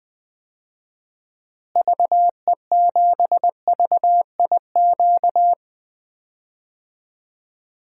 Morse code, heard as a keyed tone, about 20 wpm, 700 Hz. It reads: VE7VIQ
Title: DXCC-100 Call Signs - 6 Characters - Repeated 2x - 20wpm